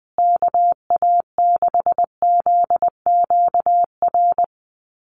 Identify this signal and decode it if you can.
Morse code, heard as a keyed tone, about 20 wpm, 700 Hz. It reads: KA6ZQR
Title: USA Call Signs Repeated 3x 20wpm